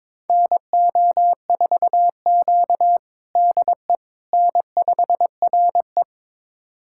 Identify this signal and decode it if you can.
Morse code, heard as a keyed tone, about 22 wpm, 700 Hz. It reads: NO4Q DE N5RE
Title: USA Callsign DE Callsign - 4 Characters - 22wpm